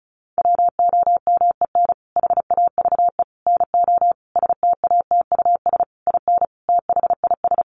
Morse code, heard as a keyed tone, about 34 wpm, 700 Hz. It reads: WOMEN HAVE NO STATUS IN THIS
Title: Sentences from Top 1400 Words 35wpm